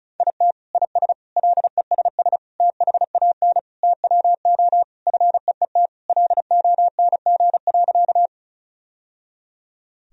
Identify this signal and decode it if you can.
Morse code, heard as a keyed tone, about 34 wpm, 700 Hz. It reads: IT IS LESS THAN TWO FEET LONG.
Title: Sentences from Top 1000 Words - Review Part 2 - 35wpm